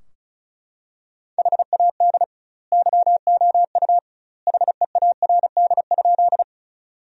Morse code, heard as a keyed tone, about 34 wpm, 700 Hz. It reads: HAD YOU HEARD?
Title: 2x Word-Spacing - Sentences from Top 400 Words - 35wpm